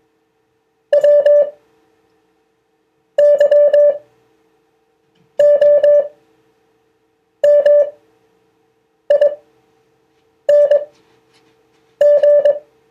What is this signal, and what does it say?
Morse code, heard as a keyed tone, about 21 wpm, spaced as for 6 wpm, 580 Hz. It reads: WYOMING